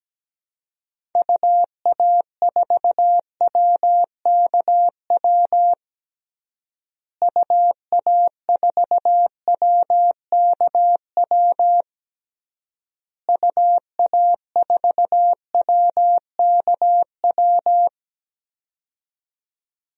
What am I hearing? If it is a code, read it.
Morse code, heard as a keyed tone, about 17 wpm, 700 Hz. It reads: UA4WKW UA4WKW UA4WKW
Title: DXCC-100 Call Signs - 6 Characters - Repeated 3x - 17wpm